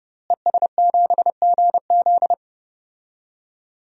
Morse code, heard as a keyed tone, about 30 wpm, 700 Hz. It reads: ES7GZ
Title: DXCC-100 Call Signs - 5 Characters - Repeated 2x - 30wpm